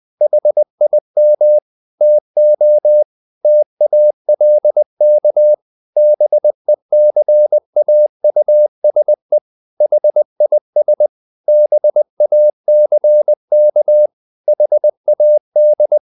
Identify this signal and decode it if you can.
Morse code, heard as a keyed tone, about 20 wpm, 600 Hz. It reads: HIM TO TALK BECAUSE HIS BACK HAD
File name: the old man and the sea / the old man and the sea part 1 20wpm